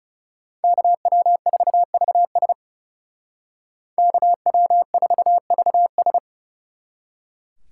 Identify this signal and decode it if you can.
Morse code, heard as a keyed tone, about 39 wpm, 700 Hz. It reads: KW4VS KW4VS